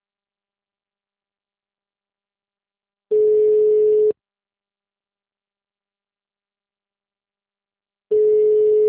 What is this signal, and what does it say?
Ringback tone (Europe, 425 Hz).
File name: out- -2031- - - .61